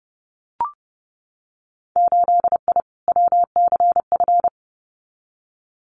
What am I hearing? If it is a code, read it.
Morse code, heard as a keyed tone, about 30 wpm, 700 Hz. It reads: 8I WCF
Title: Winter Field Day Exchanges Encore 30wpm